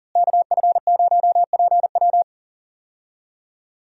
Morse code, heard as a keyed tone, about 40 wpm, 700 Hz. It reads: KF0PW
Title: USA Call Signs - 5 Characters - Repeated 3x - 40wpm